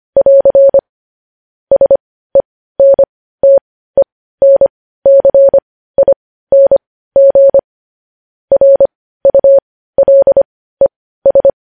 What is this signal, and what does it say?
Morse code, heard as a keyed tone, about 25 wpm, spaced as for 15 wpm, 550 Hz. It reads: <AR> SENTENCING RULES
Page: News Headlines in Morse Code at 15 WPM